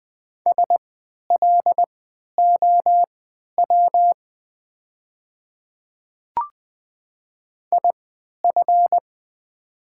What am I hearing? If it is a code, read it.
Morse code, heard as a keyed tone, about 20 wpm, spaced as for 11 wpm, 700 Hz. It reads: SLOW IF
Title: 3x Character Spacing - Top 500 Words - 20wpm